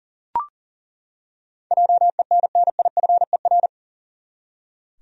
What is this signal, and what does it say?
Morse code, heard as a keyed tone, about 40 wpm, 700 Hz. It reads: JENNIFER
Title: USA Common Names - 40wpm